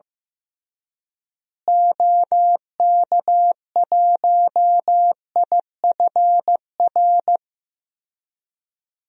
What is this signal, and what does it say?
Morse code, heard as a keyed tone, about 15 wpm, 700 Hz. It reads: OK1IFR
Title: DXCC-100 Call Signs - 6 Characters - Repeated 3x - 15wpm